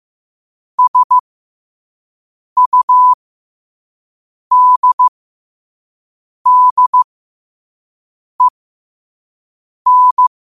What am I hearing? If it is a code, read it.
Morse code, heard as a keyed tone, about 15 wpm, spaced as for 5 wpm, 1000 Hz. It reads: SUDDEN
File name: qotdfiles / 5wpm